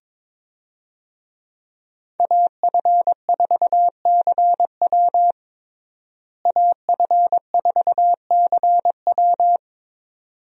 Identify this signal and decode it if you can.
Morse code, heard as a keyed tone, about 22 wpm, 700 Hz. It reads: AF4CW AF4CW
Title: USA Call Signs - 5 Characters - Repeated 2x - 22wpm